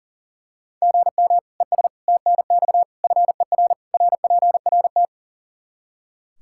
Morse code, heard as a keyed tone, about 40 wpm, 700 Hz. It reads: GM ES TNX FER RPRT
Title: Practice QSO - Exchange 1 - Beginning - 40wpm